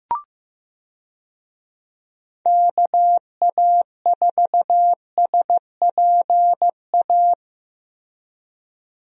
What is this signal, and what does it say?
Morse code, heard as a keyed tone, about 16 wpm, 700 Hz. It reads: KA4SPA
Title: USA Call Signs - 6 Characters - Repeated 2x - 15wpm